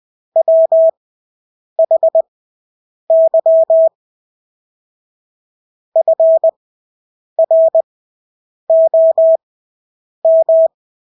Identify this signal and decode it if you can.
Morse code, heard as a keyed tone, about 20 wpm, spaced as for 8 wpm, 650 Hz. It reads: WHY FROM
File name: Engelske ord 20/8 wpm